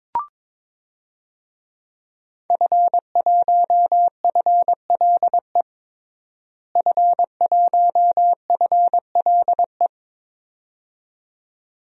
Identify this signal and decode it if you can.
Morse code, heard as a keyed tone, about 22 wpm, 700 Hz. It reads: F1FLE F1FLE
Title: DXCC-100 Call Signs - 5 Characters - Repeated 2x - 22wpm